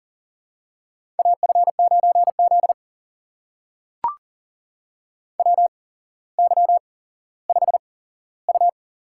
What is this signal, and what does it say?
Morse code, heard as a keyed tone, about 38 wpm, spaced as for 17 wpm, 700 Hz. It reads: AF9Z WY5U
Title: USA Call Signs - 4 Characters - 8x Character Spacing - 40wpm